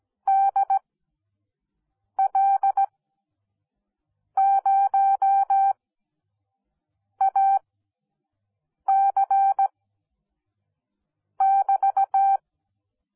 Morse code, heard as a keyed tone, about 17 wpm, spaced as for 5 wpm, 780 Hz. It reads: DL0AC<BT>